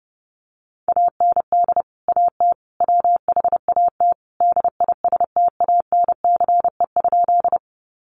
Morse code, heard as a keyed tone, about 28 wpm, 700 Hz. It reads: AND AT WHAT DISTANCE?
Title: Sentences from Top 1200 Words 30wpm